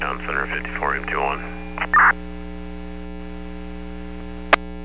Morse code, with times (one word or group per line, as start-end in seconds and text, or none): none